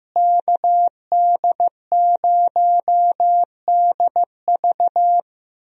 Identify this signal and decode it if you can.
Morse code, heard as a keyed tone, about 15 wpm, 700 Hz. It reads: KD0DV
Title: USA Call Signs - 5 Characters - Repeated 3x - 15wpm